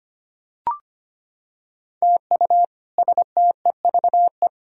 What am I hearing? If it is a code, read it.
Morse code, heard as a keyed tone, about 25 wpm, 700 Hz. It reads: TU STEVE
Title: K1USN SST QSO Practice - 25wpm